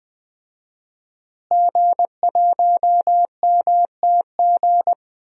Morse code, heard as a keyed tone, about 21 wpm, 700 Hz. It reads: G1MTG
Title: DXCC-100 Call Signs - 5 Characters - Repeated 2x - 20wpm